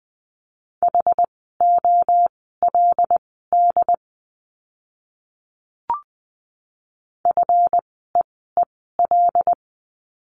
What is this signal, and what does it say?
Morse code, heard as a keyed tone, about 20 wpm, spaced as for 14 wpm, 700 Hz. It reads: HOLD FEEL
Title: 237 Lesson 38f - Letter q - Words 20wpm